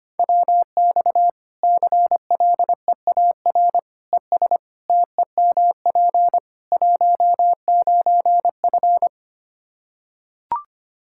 Morse code, heard as a keyed tone, about 25 wpm, 700 Hz. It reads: WX CLEAR ES TEMP 19F
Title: Practice QSO - Exchange 2 - Info - 25wpm